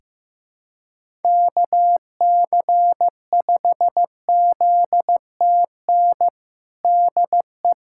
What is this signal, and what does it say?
Morse code, heard as a keyed tone, about 15 wpm, 700 Hz. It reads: KC5ZTN DE
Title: USA Callsign DE Callsign 15wpm